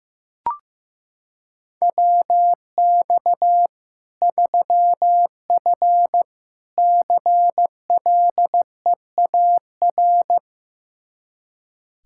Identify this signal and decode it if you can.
Morse code, heard as a keyed tone, about 15 wpm, 700 Hz. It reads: WX 3F CLEAR
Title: K1USN SST QSO Practice 15wpm